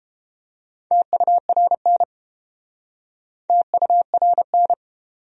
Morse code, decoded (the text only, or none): TURN TURN